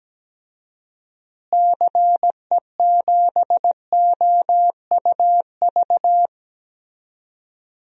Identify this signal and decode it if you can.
Morse code, heard as a keyed tone, about 17 wpm, 700 Hz. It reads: CE7OUV
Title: DXCC-100 Call Signs - 6 Characters - Repeated 3x - 17wpm